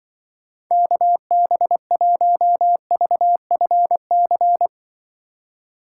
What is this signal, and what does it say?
Morse code, heard as a keyed tone, about 25 wpm, 700 Hz. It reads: KB1VFC
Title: Speed Racing - USA 2x3 Call Signs 20wpm